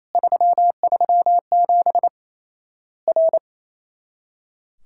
Morse code, heard as a keyed tone, about 28 wpm, 700 Hz. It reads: 337 R
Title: Worked All Europe DX Contest - QTC Exchange - 28wpm